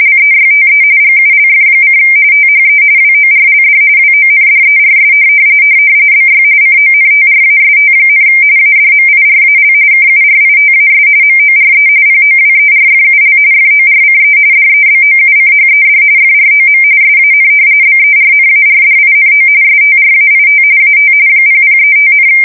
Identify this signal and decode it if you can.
RTTY, about 45.45 baud. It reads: LEYTE GULF: THE WORLD WONDERS BY CONTRAST, ADMIRAL KURITAS MORE POWERFUL FORCE DEBOUCHED FROM SAN BERNARDINO STRAIT AT 0035,
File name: fsk170 / battleshipnj